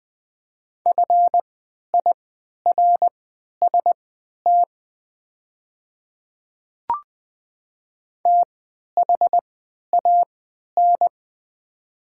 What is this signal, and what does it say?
Morse code, heard as a keyed tone, about 21 wpm, spaced as for 11 wpm, 700 Hz. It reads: FIRST THAN
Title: 3x Character Spacing - Top 100 Words - 20wpm